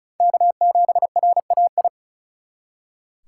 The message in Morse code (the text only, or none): K7RAI